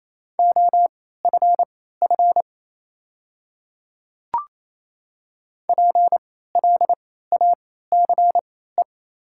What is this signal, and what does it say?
Morse code, heard as a keyed tone, about 28 wpm, spaced as for 16 wpm, 690 Hz. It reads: OFF PLACE
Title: 3x Character Spacing - Top 200 Words - 28wpm